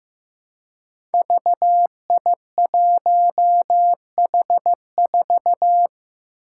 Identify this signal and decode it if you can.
Morse code, heard as a keyed tone, about 15 wpm, 700 Hz. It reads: VI1H4
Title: Five Challenging Letter-Number Combinations 15wpm